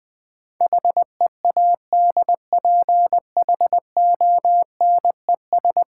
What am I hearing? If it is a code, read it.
Morse code, heard as a keyed tone, about 20 wpm, 700 Hz. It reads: HEADPHONES
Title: Easy Words with Hints - 20wpm